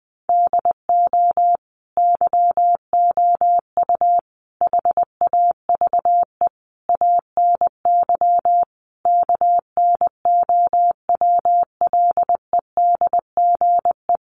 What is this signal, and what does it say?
Morse code, heard as a keyed tone, about 19 wpm, 700 Hz. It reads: DO YOU HAVE ANY KNOWLEDGE
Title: Sentences from Top 600 Words 20wpm